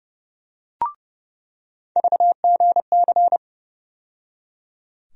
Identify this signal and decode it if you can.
Morse code, heard as a keyed tone, about 29 wpm, 700 Hz. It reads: VGC
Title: USA Call Sign Postfixes - 3 Letters 30wpm